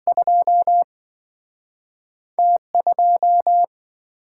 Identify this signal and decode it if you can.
Morse code, heard as a keyed tone, about 28 wpm, spaced as for 24 wpm, 700 Hz. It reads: 2 T2
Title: Speed-Racing - Two Letter-Number Combinations - 20wpm